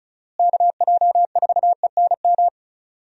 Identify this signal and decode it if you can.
Morse code, heard as a keyed tone, about 36 wpm, 700 Hz. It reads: KJ4ENM
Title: USA Call Signs - 6 Characters - Repeated 2x - 35wpm